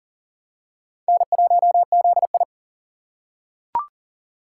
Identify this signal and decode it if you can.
Morse code, heard as a keyed tone, about 40 wpm, 700 Hz. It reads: N1ZI
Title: USA Call Signs - 2x Character Spacing - 40wpm